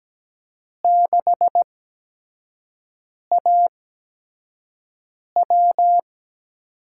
Morse code, heard as a keyed tone, about 17 wpm, spaced as for 5 wpm, 700 Hz. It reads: 6AW
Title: USA Call Signs - 4 Characters - 8x Character Spacing - 17wpm